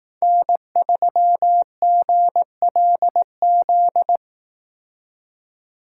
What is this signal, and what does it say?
Morse code, heard as a keyed tone, about 18 wpm, 700 Hz. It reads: N3GLZ
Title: Speed Racing - USA Call Signs 15wpm